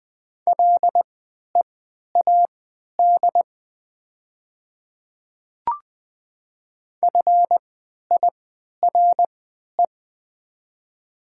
Morse code, heard as a keyed tone, about 20 wpm, spaced as for 11 wpm, 700 Hz. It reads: LEAD FIRE
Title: 3x Character Spacing - Top 500 Words - 20wpm